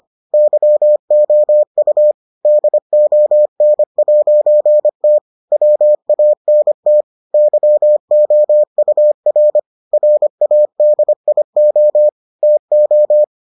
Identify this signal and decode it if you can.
Morse code, heard as a keyed tone, about 25 wpm, 600 Hz. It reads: YOU DON'T WANT YOUR RADIO TO